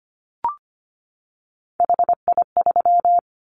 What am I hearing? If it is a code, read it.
Morse code, heard as a keyed tone, about 25 wpm, 700 Hz. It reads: HI3
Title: Callsign Prefixes for Top 100 DXCC - 25wpm